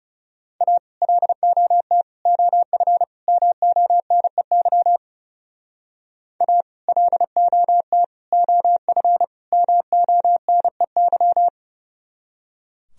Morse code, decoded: A LOT OF MONEY A LOT OF MONEY

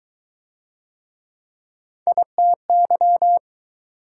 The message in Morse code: ITY

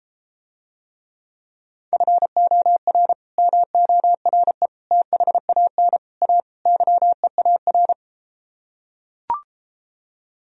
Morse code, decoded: FOR MORE THAN A YEAR

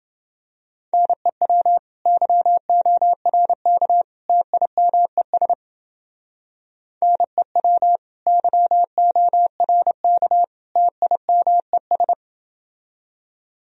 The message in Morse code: NEW YORK TIMES NEW YORK TIMES

